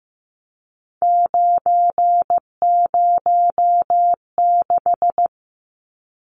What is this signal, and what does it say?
Morse code, heard as a keyed tone, about 15 wpm, 700 Hz. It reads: 906
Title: Numbers - 3-Digit 15wpm